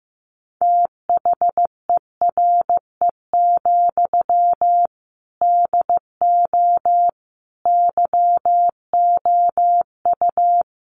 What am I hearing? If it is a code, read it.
Morse code, heard as a keyed tone, about 15 wpm, 700 Hz. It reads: THERE, DO YOU
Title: Sentences from Top 300 Words 15wpm